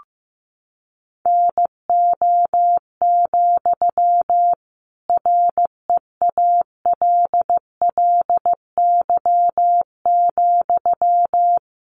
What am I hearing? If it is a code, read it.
Morse code, heard as a keyed tone, about 15 wpm, 700 Hz. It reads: NO, REALLY,
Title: Sentences from Top 1000 Words 15wpm